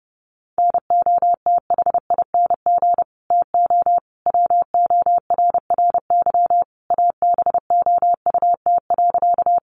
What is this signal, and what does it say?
Morse code, heard as a keyed tone, about 30 wpm, 700 Hz. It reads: NOTHING TO WORRY ABOUT.
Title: Sentences from Top 1100 Words 30wpm